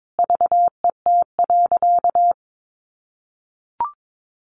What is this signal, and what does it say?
Morse code, heard as a keyed tone, about 22 wpm, 700 Hz. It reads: VET.
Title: Easy Phrases 22wpm